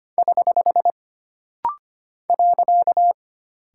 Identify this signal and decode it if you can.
Morse code, heard as a keyed tone, about 25 wpm, 700 Hz. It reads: <HH> .
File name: Single Punctuation-Prosign - Rapid-Fire - 25wpm